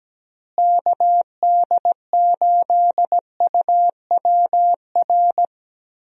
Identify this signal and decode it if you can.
Morse code, heard as a keyed tone, about 17 wpm, 700 Hz. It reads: KD8UWR
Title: USA Call Signs - 6 Characters - Repeated 2x - 17wpm